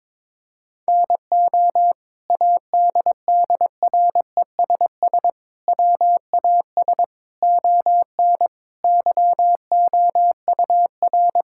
Morse code, heard as a keyed tone, about 22 wpm, 700 Hz. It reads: NO ADDRESS WAS ON YOUR
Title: Sentences from Top 1000 Words - Review Part 6 - 22wpm